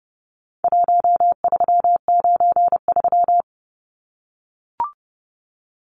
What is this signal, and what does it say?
Morse code, heard as a keyed tone, about 30 wpm, 700 Hz. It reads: 1393
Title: Numbers - 4-Digit 30wpm